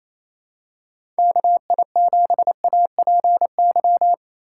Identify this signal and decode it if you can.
Morse code, heard as a keyed tone, about 28 wpm, 700 Hz. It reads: KI7APY
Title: USA Call Signs - 6 Characters - 4x Character Spacing - 28wpm